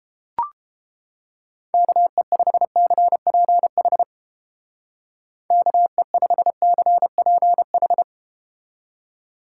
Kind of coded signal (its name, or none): Morse code